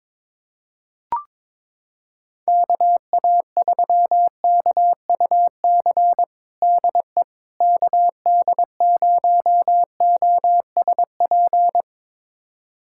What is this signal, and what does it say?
Morse code, heard as a keyed tone, about 22 wpm, 700 Hz. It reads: KA3KUC DE KD0OSP
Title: USA Callsign DE Callsign 22wpm